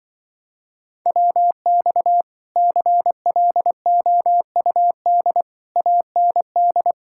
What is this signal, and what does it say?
Morse code, heard as a keyed tone, about 24 wpm, 700 Hz. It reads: WX CLOUD AND
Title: Speed-Racing Common QSO Phrases - 20wpm